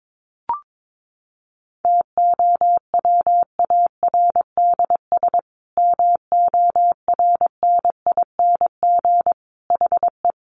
Morse code, heard as a keyed tone, about 22 wpm, 700 Hz. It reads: TOWARDS MORNING HE
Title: Sentences from Top 1100 Words 22wpm